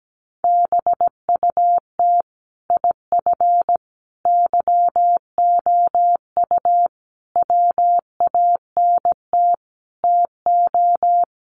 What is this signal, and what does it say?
Morse code, heard as a keyed tone, about 17 wpm, 700 Hz. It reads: BUT IF YOU WANT TO
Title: Sets of 5 Words Encore2 17wpm